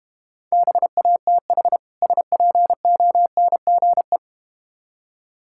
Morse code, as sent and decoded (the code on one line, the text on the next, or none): -... .- - .... / ... .--. --- -. --. .
BATH SPONGE